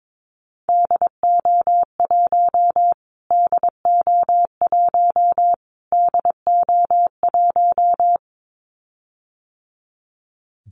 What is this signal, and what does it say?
Morse code, heard as a keyed tone, about 22 wpm, 700 Hz. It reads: DO1 DO1 DO1
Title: Callsign Prefixes for Top 25 DXCC - Repeated 3x - 22wpm